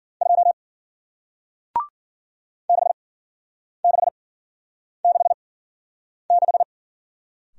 Morse code, heard as a keyed tone, about 54 wpm, spaced as for 9 wpm, 700 Hz. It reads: 2 6666